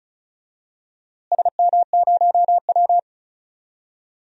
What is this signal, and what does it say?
Morse code, heard as a keyed tone, about 34 wpm, 700 Hz. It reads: SM0W
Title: DXCC-100 Call Signs - 4 Characters - Repeated 3x - 35wpm